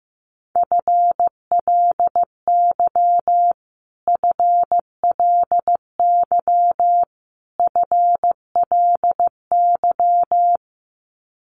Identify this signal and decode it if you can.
Morse code, heard as a keyed tone, about 15 wpm, 700 Hz. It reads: FLY FLY FLY
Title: Top 500 Words Repeated 15wpm